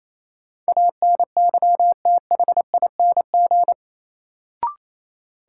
Morse code, heard as a keyed tone, about 28 wpm, 700 Hz. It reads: ANYTHING
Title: Sets of 4 Words - New Word Drill - 28wpm